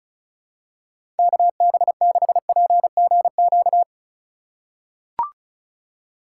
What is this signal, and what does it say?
Morse code, heard as a keyed tone, about 34 wpm, 700 Hz. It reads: KB6PGQ
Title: USA Call Signs - 3x Character Spacing - 35wpm